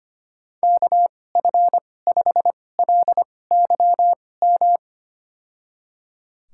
Morse code, heard as a keyed tone, about 25 wpm, spaced as for 18 wpm, 700 Hz. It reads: KF5LYM
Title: USA Call Signs - 6 Characters - 2x Character Spacing - 25wpm